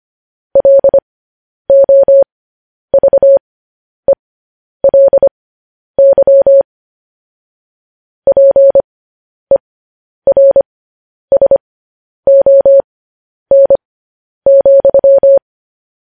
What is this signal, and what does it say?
Morse code, heard as a keyed tone, about 25 wpm, spaced as for 10 wpm, 550 Hz. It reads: LOVELY PERSON,
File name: podcast / News headlines-1x-25x10